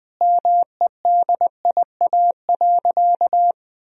Morse code, heard as a keyed tone, about 20 wpm, 700 Hz. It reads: MEDIA.
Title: Sentences from Top 1000 Words - Review Part 4 - 20wpm